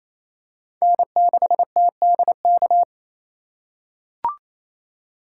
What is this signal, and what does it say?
Morse code, heard as a keyed tone, about 28 wpm, 700 Hz. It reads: N6TDK
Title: USA Call Signs - 5 Characters - 4x Character Spacing - 28wpm